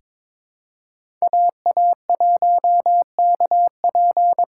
Morse code, heard as a keyed tone, about 22 wpm, 700 Hz. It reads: AA1KP
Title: USA Call Signs - 5 Characters - 4x Character Spacing - 22wpm